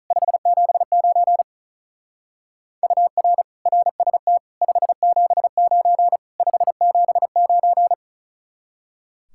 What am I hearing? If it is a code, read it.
Morse code, heard as a keyed tone, about 44 wpm, 700 Hz. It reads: 579 UR RST 579 579
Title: Speed-Racing Common QSO Phrases Encore - 35wpm